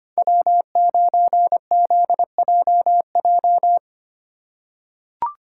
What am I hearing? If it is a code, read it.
Morse code, heard as a keyed tone, about 25 wpm, 700 Hz. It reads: W9ZJJ
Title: USA Call Signs - 5 Characters - 3x Character Spacing - 25wpm